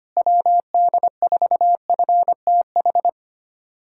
Morse code, heard as a keyed tone, about 25 wpm, 700 Hz. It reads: WD4FTH
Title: USA Callsign DE Callsign - 6 Characters - 25wpm